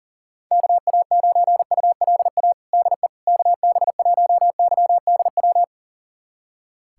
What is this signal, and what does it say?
Morse code, heard as a keyed tone, about 40 wpm, 700 Hz. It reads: KA9ULA DE KB1YDW
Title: USA Callsign DE Callsign - 6 Characters - 40wpm